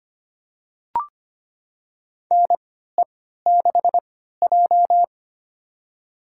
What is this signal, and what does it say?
Morse code, heard as a keyed tone, about 25 wpm, spaced as for 14 wpm, 700 Hz. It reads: NE6J
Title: USA Call Signs - 4 Characters - 3x Character Spacing - 25wpm